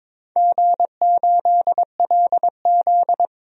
Morse code, heard as a keyed tone, about 22 wpm, 700 Hz. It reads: G8LZ
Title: DXCC-100 Call Signs - 4 Characters - 22wpm